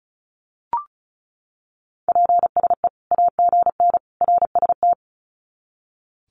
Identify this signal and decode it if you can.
Morse code, heard as a keyed tone, about 34 wpm, 700 Hz. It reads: PSE AGN RST
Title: Common QSO Phrases 35wpm